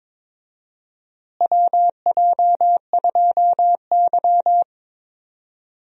Morse code, heard as a keyed tone, about 21 wpm, 700 Hz. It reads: WJ2Y
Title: USA Call Signs - 4 Characters - Repeated 2x - 22wpm